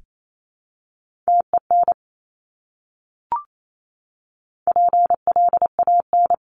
Morse code, heard as a keyed tone, about 28 wpm, 700 Hz. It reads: TEN PLAN
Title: Top 500 Words Repeated 28wpm